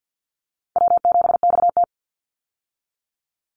Morse code, heard as a keyed tone, about 50 wpm, 700 Hz. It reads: W7XT
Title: Easy Call Signs Repeated 2x - 50wpm